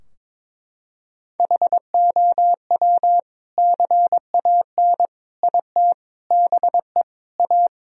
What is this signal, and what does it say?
Morse code, heard as a keyed tone, about 22 wpm, 700 Hz. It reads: HOW CAN IT BE A
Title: Sentences from Top 1000 Words - Review Part 2 - 22wpm